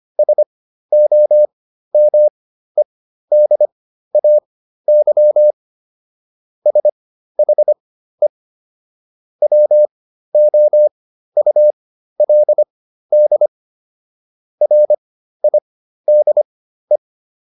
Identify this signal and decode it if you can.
Morse code, heard as a keyed tone, about 25 wpm, spaced as for 13 wpm, 600 Hz. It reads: SOMEDAY SHE WOULD RIDE